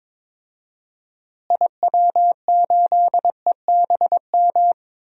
Morse code, heard as a keyed tone, about 22 wpm, 700 Hz. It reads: IW8EBM
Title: DXCC-100 Call Signs - 6 Characters - Repeated 3x - 22wpm